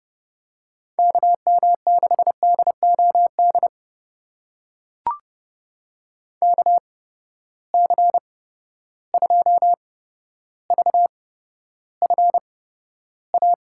Morse code, heard as a keyed tone, about 30 wpm, spaced as for 13 wpm, 700 Hz. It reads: KM6DOD KC2VFA
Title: USA Call Signs - 6 Characters - 8x Character Spacing - 30wpm